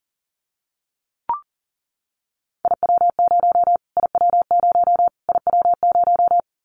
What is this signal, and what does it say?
Morse code, heard as a keyed tone, about 40 wpm, 700 Hz. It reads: IW0 IW0 IW0
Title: Callsign Prefixes for Top 25 DXCC - Repeated 3x - 40wpm